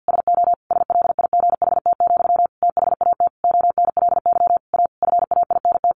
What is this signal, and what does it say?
Morse code, heard as a keyed tone, about 50 wpm, 700 Hz. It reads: SO SLIGHT, THAT ONLY A FAINT
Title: A Sound Of Thunder 50wpm